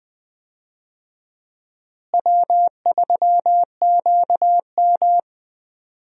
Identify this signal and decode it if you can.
Morse code, heard as a keyed tone, about 20 wpm, 700 Hz. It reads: W3QM
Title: USA Call Signs - 4 Characters - Repeated 3x - 20wpm